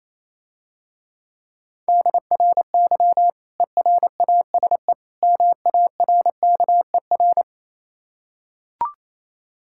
Morse code, heard as a keyed tone, about 28 wpm, 700 Hz. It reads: DRY ERASE MARKER